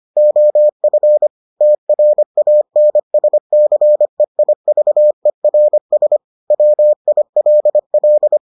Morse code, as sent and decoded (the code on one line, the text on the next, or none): --- ..-. / - .-. .- -. ... -.-. .. ...- . .-. ... / .-- .. .-.. .-..
OF TRANSCEIVERS WILL